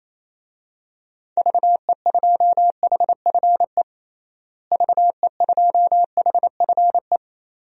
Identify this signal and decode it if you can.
Morse code, heard as a keyed tone, about 28 wpm, 700 Hz. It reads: VE2HFE VE2HFE
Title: DXCC-100 Call Signs - 6 Characters - Repeated 2x - 28wpm